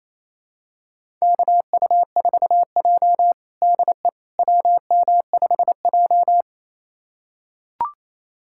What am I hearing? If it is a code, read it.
Morse code, heard as a keyed tone, about 28 wpm, 700 Hz. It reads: KU4J DE WM5J
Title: USA Callsign DE Callsign - 4 Characters - 28wpm